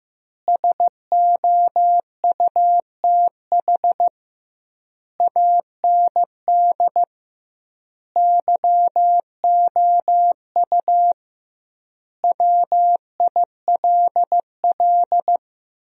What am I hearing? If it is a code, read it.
Morse code, heard as a keyed tone, about 15 wpm, 700 Hz. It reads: SOUTH AND YOU WILL
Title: 2x Word-Spacing - Sentences from Top 400 Words - 15wpm